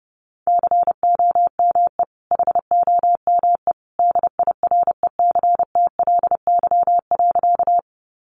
Morse code, heard as a keyed tone, about 30 wpm, 700 Hz. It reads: COME HOME DIRECTLY.
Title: Sentences from Top 800 Words 30wpm